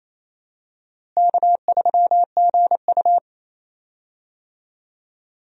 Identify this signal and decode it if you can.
Morse code, heard as a keyed tone, about 27 wpm, 700 Hz. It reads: K3GU
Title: USA Call Signs - 4 Characters - Repeated 3x - 28wpm